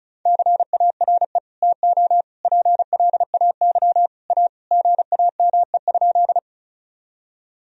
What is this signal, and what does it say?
Morse code, heard as a keyed tone, about 36 wpm, 700 Hz. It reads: CARE TO PLAY A GAME?
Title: Sentences from Top 1000 Words - Review Part 1 - 35wpm